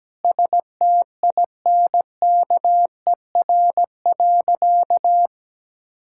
Morse code, heard as a keyed tone, about 17 wpm, 700 Hz. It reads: STINKER.
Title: Pangram Challenge v2 - 17wpm